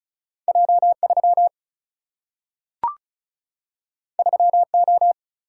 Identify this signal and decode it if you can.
Morse code, heard as a keyed tone, about 36 wpm, 700 Hz. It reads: J3 3O
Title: Single-Letter Single-Number 35wpm